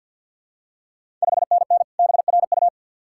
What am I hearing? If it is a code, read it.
Morse code, heard as a keyed tone, about 50 wpm, 690 Hz. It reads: 5NN BRU
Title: Ontario QSO Party Exchange 50wpm